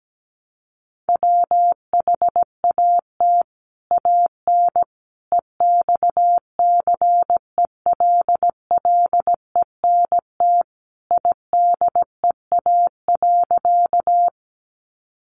Morse code, heard as a keyed tone, about 17 wpm, 700 Hz. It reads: WHAT AN EXCELLENT IDEA.
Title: Sentences from Top 1000 Words 17wpm